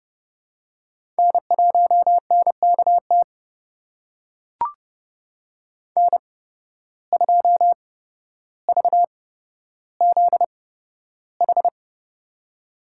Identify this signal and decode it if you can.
Morse code, heard as a keyed tone, about 30 wpm, spaced as for 13 wpm, 700 Hz. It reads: N1NKT N2VZH
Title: USA Call Signs - 5 Characters - 8x Character Spacing - 30wpm